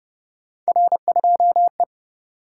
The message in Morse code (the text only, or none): R2E